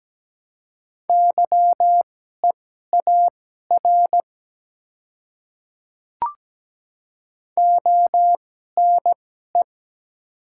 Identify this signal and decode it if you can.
Morse code, heard as a keyed tone, about 17 wpm, spaced as for 12 wpm, 700 Hz. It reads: YEAR ONE
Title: 2x Character Spacing - Top 200 Words - 17wpm